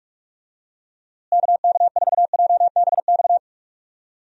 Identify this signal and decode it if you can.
Morse code, heard as a keyed tone, about 42 wpm, 700 Hz. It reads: KK4JBX
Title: USA Call Signs - 6 Characters - Repeated 2x - 45wpm